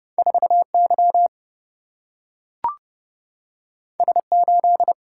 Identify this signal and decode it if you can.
Morse code, heard as a keyed tone, about 30 wpm, 700 Hz. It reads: 4Y S8
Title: Single-Letter Single-Number 30wpm